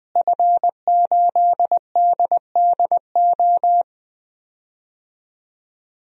Morse code, heard as a keyed tone, about 20 wpm, 700 Hz. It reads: F8DDO
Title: DXCC-100 Call Signs - 5 Characters - Repeated 3x - 20wpm